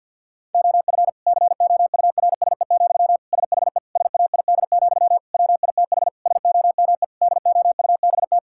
Morse code, heard as a keyed tone, about 50 wpm, 700 Hz. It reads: OF COURSE, SHE SAID, WITH SOME DOUBT
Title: Sentences from Top 1600 Words 50wpm